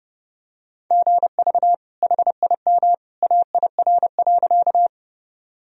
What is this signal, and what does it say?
Morse code, heard as a keyed tone, about 32 wpm, 700 Hz. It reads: GV HIM AIR.
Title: Sentences from Top 200 Words and Abbreviations 30wpm